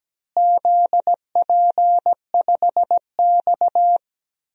Morse code, heard as a keyed tone, about 17 wpm, 700 Hz. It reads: ZP5X
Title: DXCC-100 Call Signs - 4 Characters - Repeated 2x - 17wpm